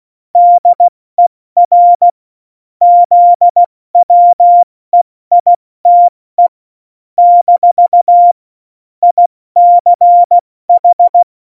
Morse code, heard as a keyed tone, about 16 wpm, 700 Hz. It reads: DER ZWEITE - ICH
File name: Morsekurs / Witze / Witz 01. September 2020, 70 F80